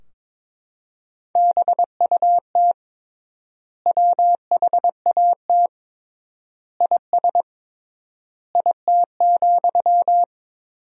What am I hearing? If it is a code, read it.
Morse code, heard as a keyed tone, about 22 wpm, 700 Hz. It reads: BUT WHAT IS IT,
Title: 3x Word-Spacing - Sentences from Top 300 Words - 22wpm